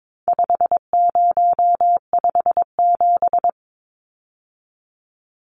Morse code, heard as a keyed tone, about 22 wpm, 700 Hz. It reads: 5057
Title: Numbers - 4-Digit 22wpm